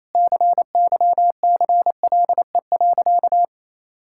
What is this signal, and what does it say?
Morse code, heard as a keyed tone, about 27 wpm, 700 Hz. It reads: CYCLE.
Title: Sentences from Top 1000 Words - Review Part 6 - 28wpm